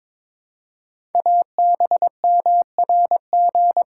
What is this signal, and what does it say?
Morse code, heard as a keyed tone, about 23 wpm, 700 Hz. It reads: ABMRG
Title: Code Groups - 5 Letters - 22wpm